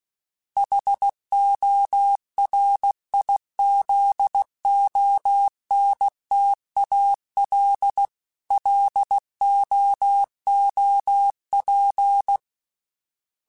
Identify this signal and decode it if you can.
Morse code, heard as a keyed tone, about 16 wpm, 800 Hz. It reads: HORIZONTAL LOOP
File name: fromSMs / FromSMs mp3s / Antennas@15